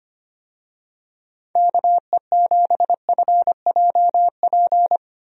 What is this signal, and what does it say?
Morse code, heard as a keyed tone, about 25 wpm, 700 Hz. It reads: KE7FJP
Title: USA Call Signs - 6 Characters - Repeated 3x - 25wpm